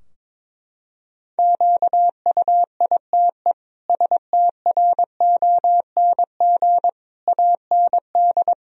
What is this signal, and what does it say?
Morse code, heard as a keyed tone, about 22 wpm, 700 Hz. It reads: QUITE STRONG AND
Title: Sentences from Top 1000 Words - Review Part 3 - 22wpm